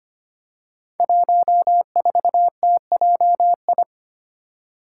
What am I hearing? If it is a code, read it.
Morse code, heard as a keyed tone, about 25 wpm, 700 Hz. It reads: 14TJI
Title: Five Challenging Letter-Number Combinations 25wpm